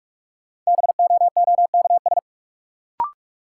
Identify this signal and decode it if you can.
Morse code, heard as a keyed tone, about 46 wpm, 700 Hz. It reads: BOOKS
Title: Sentences from Top 1000 Words - Review Part 4 - New Word Drill - 45wpm